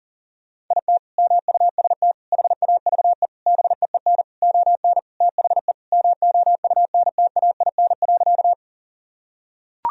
Morse code, heard as a keyed tone, about 37 wpm, 700 Hz. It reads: IT MUST HAVE BEEN ON THE MOUNTAIN.
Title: Sentences from Top 1000 Words - Review Part 1 - 40wpm